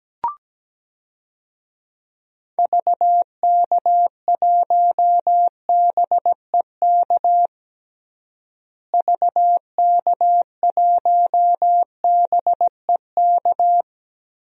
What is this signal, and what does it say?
Morse code, heard as a keyed tone, about 17 wpm, 700 Hz. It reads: VK1BEK VK1BEK
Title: DXCC-100 Call Signs - 6 Characters - Repeated 3x - 17wpm